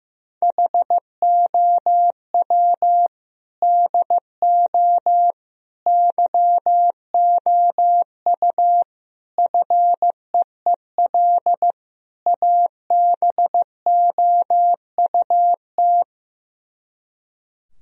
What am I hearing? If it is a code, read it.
Morse code, heard as a keyed tone, about 15 wpm, 700 Hz. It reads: HOW DO YOU FEEL ABOUT
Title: Speed-Racing - Sets of 5 Words 15wpm